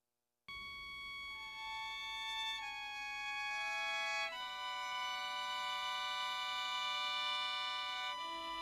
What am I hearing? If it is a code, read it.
Dial-up modem handshake.